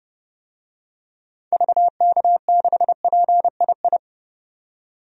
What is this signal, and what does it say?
Morse code, heard as a keyed tone, about 30 wpm, 700 Hz. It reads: VK6PII
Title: DXCC-100 Call Signs - 6 Characters - 30wpm